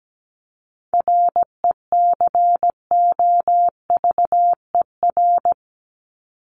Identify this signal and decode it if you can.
Morse code, heard as a keyed tone, about 17 wpm, 700 Hz. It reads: RECOVER
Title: Top 1751-2000 Words 17wpm